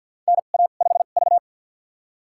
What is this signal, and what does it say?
Morse code, heard as a keyed tone, about 52 wpm, spaced as for 36 wpm, 700 Hz. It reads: NA5V
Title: USA Call Signs - 2x Character Spacing - 50wpm